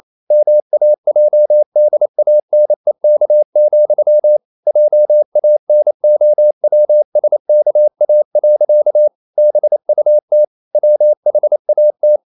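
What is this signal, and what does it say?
Morse code, heard as a keyed tone, about 28 wpm, 600 Hz. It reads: MAJDANEK, JANOWSKA. BUT WHAT
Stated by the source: Morse code practice at 28 wpm, text of Ken Follett's novel Never in English, from CW med Gnister.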